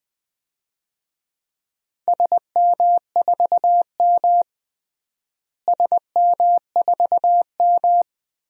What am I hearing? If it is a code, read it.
Morse code, heard as a keyed tone, about 19 wpm, 700 Hz. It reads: SM4M SM4M
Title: DXCC-100 Call Signs - 4 Characters - Repeated 3x - 20wpm